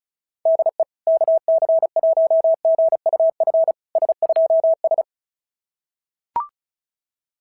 Morse code, heard as a keyed tone, about 36 wpm, 650 Hz. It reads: DE KC1GUF S2S
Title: SOTA QSO Practice 35wpm